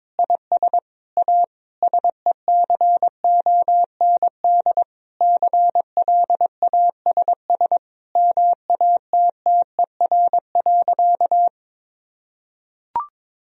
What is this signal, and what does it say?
Morse code, heard as a keyed tone, about 21 wpm, 700 Hz. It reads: IS A SECOND CLASS MATTER.
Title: Sentences from Top 1000 Words - Review Part 2 - 22wpm